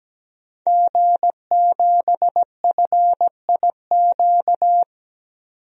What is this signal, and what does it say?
Morse code, heard as a keyed tone, about 17 wpm, 700 Hz. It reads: G7FIQ
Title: DXCC-100 Call Signs - 5 Characters - 17wpm